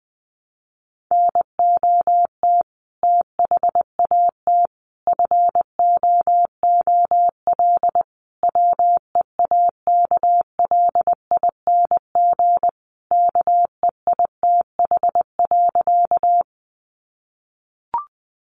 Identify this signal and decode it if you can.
Morse code, heard as a keyed tone, about 19 wpm, 700 Hz. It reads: NOT THAT FOOL WEAKLING KEITH.
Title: A Sound Of Thunder 20wpm